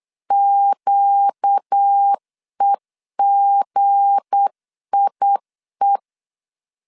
Morse code, heard as a keyed tone, about 8 wpm, 800 Hz. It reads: QEGIE
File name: LEZIONI / 7 GF9Z / 7 G 8wpm 3118